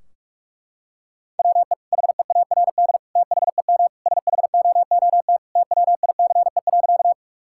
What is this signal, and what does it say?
Morse code, heard as a keyed tone, about 46 wpm, 700 Hz. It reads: WE HEARD THEM SHOOT TWICE.
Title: Sentences from Top 1600 Words 45wpm